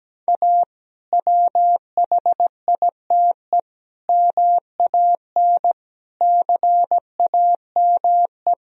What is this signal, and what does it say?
Morse code, heard as a keyed tone, about 17 wpm, 700 Hz. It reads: A WHITE MAN CAME
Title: Sentences from Top 1000 Words - Review Part 1 - 17wpm